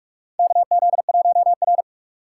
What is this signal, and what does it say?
Morse code, heard as a keyed tone, about 44 wpm, 700 Hz. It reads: KZ1R
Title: USA Call Signs - 4x Character Spacing - 45wpm